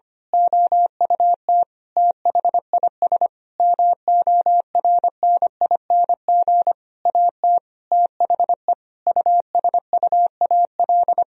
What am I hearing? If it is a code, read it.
Morse code, heard as a keyed tone, about 25 wpm, 700 Hz. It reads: OUT THIS MORNING AT THE USUAL
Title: Sentences from Top 1000 Words - Review Part 2 - 25wpm